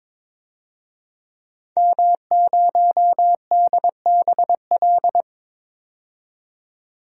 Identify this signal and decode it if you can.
Morse code, heard as a keyed tone, about 22 wpm, 700 Hz. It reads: M0DBL
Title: DXCC-100 Call Signs - 5 Characters - 22wpm